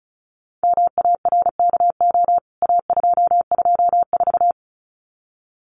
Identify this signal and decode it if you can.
Morse code, heard as a keyed tone, about 34 wpm, 700 Hz. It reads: MARKO A224
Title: CWT Exchange 35wpm